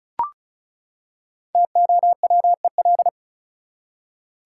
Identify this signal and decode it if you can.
Morse code, heard as a keyed tone, about 34 wpm, 700 Hz. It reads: TOWEL
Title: Five Letter Words - 35wpm